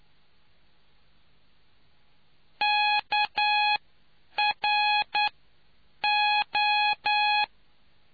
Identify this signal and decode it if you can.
Morse code, heard as a keyed tone, about 9 wpm, spaced as for 7 wpm, 840 Hz. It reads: KRO